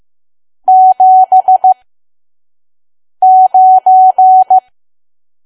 Morse code, heard as a keyed tone, about 15 wpm, spaced as for 5 wpm, 750 Hz. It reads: 79
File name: Archive / 5 WPM files / PM